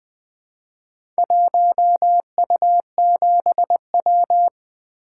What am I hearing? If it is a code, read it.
Morse code, heard as a keyed tone, about 20 wpm, 700 Hz. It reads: 1U7W
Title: Four Letter-Number Combinations - 20wpm